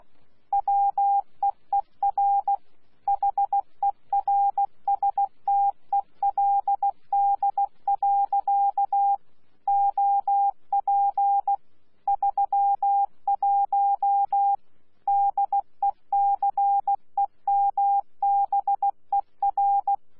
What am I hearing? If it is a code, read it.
Morse code, heard as a keyed tone, about 16 wpm, 800 Hz. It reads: WEER HERSTELD. OP 31 DECEMBER
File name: Z PI4VRZ / PI4VRZ 14-jun-2025